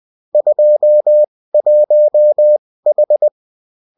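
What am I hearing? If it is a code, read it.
Morse code, heard as a keyed tone, about 20 wpm, spaced as for 16 wpm, 600 Hz. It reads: 21H
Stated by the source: Bulletin in text and CW form.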